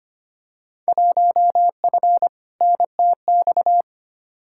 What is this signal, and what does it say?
Morse code, heard as a keyed tone, about 25 wpm, 700 Hz. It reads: 1F NTX
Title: Field Day Exchanges Encore - 25wpm